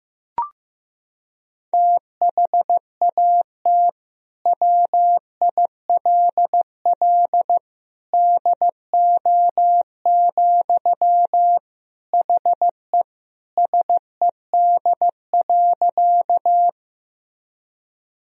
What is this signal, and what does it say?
Morse code, heard as a keyed tone, about 16 wpm, 700 Hz. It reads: THAT WILL DO, HE SED.
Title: Sentences from Top 100 Words and Abbreviations 15wpm